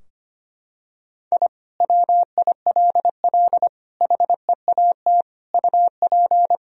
Morse code, heard as a keyed tone, about 25 wpm, 700 Hz. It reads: I WILL HEAT UP
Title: Sentences from Top 1000 Words - Review Part 3 - 25wpm